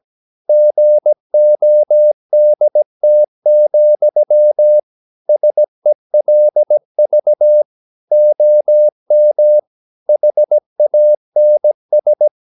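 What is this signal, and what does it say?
Morse code, heard as a keyed tone, about 17 wpm, 600 Hz. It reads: GODT, SELV OM HANS
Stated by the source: Slow (QRS) Morse practice at 17 wpm, from CW med Gnister.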